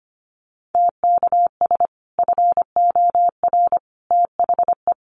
Morse code, heard as a keyed tone, about 25 wpm, 700 Hz. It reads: TKS FOR THE
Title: Common QSO Phrases 25wpm